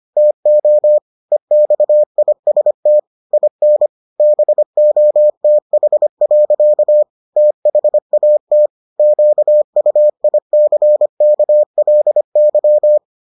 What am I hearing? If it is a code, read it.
Morse code, heard as a keyed tone, about 25 wpm, 600 Hz. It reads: TO EXIST IN BOTH. THAT QUICKLY